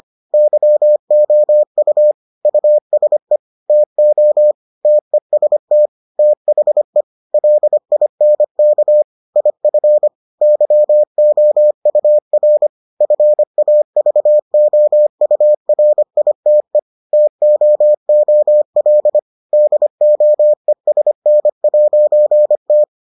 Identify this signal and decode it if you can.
Morse code, heard as a keyed tone, about 25 wpm, 600 Hz. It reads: YOU USE TO TEST THE LINK IF YOUR FAVOURITE TOOL DOESN'T